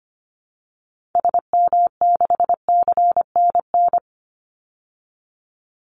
Morse code, heard as a keyed tone, about 25 wpm, 700 Hz. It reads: SM6CNN
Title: Call Signs Repeated 3x - 25wpm